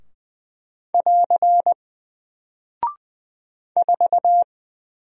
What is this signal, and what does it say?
Morse code, heard as a keyed tone, about 20 wpm, 700 Hz. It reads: <AR> 4
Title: Single Character (Letter:Number:Punctuation:Prosign) Alt 20wpm